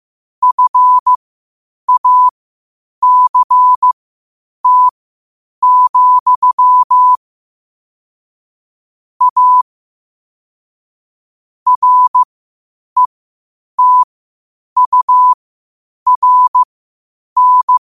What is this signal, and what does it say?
Morse code, heard as a keyed tone, about 15 wpm, spaced as for 8 wpm, 1000 Hz. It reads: FACT, A RETURN
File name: qotdfiles / 8wpm